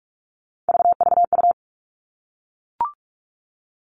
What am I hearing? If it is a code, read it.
Morse code, heard as a keyed tone, about 44 wpm, 700 Hz. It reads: VVU
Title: All Cut Numbers - 3-Digit 45wpm